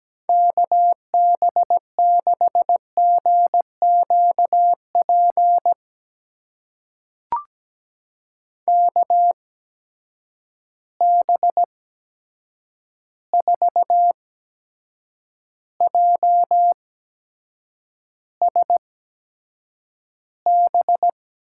Morse code, heard as a keyed tone, about 17 wpm, spaced as for 7 wpm, 700 Hz. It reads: KB6GQP KB4JSB
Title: USA Call Signs - 6 Characters - 8x Character Spacing - 17wpm